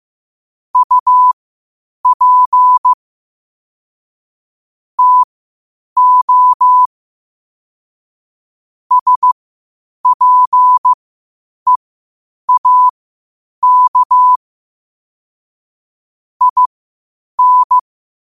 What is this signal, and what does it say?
Morse code, heard as a keyed tone, about 15 wpm, spaced as for 8 wpm, 1000 Hz. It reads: UP TO SPEAK IN
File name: qotdfiles / 8wpm